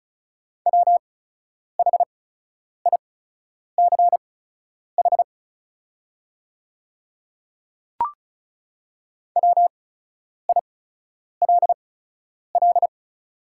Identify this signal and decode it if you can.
Morse code, heard as a keyed tone, about 36 wpm, spaced as for 10 wpm, 700 Hz. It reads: WHICH WILL
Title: 8x Character Spacing - Top 100 Words - 35wpm